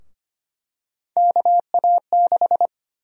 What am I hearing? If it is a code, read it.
Morse code, heard as a keyed tone, about 25 wpm, 700 Hz. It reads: KA6
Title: USA Call Sign Prefixes - Repeated 2x - 25wpm